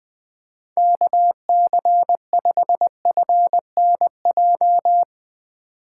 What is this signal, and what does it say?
Morse code, heard as a keyed tone, about 21 wpm, 700 Hz. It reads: KC5FNJ
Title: USA Call Signs - 6 Characters - 3x Character Spacing - 20wpm